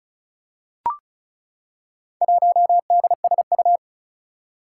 Morse code, heard as a keyed tone, about 36 wpm, 700 Hz. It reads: 1DSU
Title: Four Challenging Letter-Number Combinations 35wpm